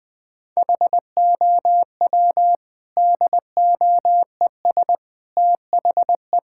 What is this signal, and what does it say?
Morse code, heard as a keyed tone, about 21 wpm, 700 Hz. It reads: HOW DOES THE
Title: Sentences from Top 1000 Words - Review Part 5 - 20wpm